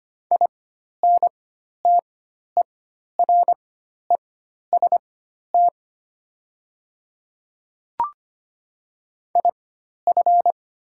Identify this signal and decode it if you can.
Morse code, heard as a keyed tone, about 25 wpm, spaced as for 12 wpm, 700 Hz. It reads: INTEREST IF
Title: 4x Character Spacing - Top 500 Words - 25wpm